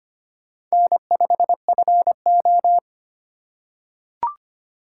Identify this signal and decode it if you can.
Morse code, heard as a keyed tone, about 25 wpm, 700 Hz. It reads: N5FO
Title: USA Call Signs - 4x Character Spacing - 25wpm